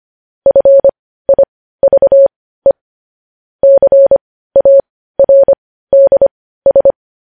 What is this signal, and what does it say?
Morse code, heard as a keyed tone, about 25 wpm, spaced as for 15 wpm, 550 Hz. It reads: FIVE CARDS
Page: news headlines in morse code at 15 wpm